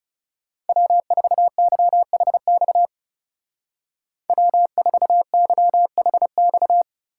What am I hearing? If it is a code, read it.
Morse code, heard as a keyed tone, about 37 wpm, 700 Hz. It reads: W4YHX W4YHX
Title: Speed Racing - USA Call Signs 30wpm